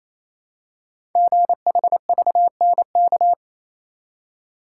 Morse code, heard as a keyed tone, about 27 wpm, 700 Hz. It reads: GHVNK
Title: Code Groups - 5 Letters - 28wpm